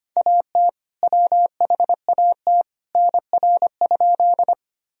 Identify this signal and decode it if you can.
Morse code, heard as a keyed tone, about 25 wpm, 700 Hz. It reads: AT WHAT NR?
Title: Sentences from Top 100 Words and Abbreviations 25wpm